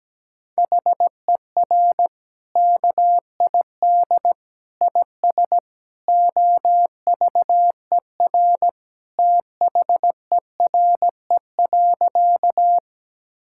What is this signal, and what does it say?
Morse code, heard as a keyed tone, about 17 wpm, 700 Hz. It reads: HER KID IS OVER THERE.
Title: Sentences from Top 1600 Words 17wpm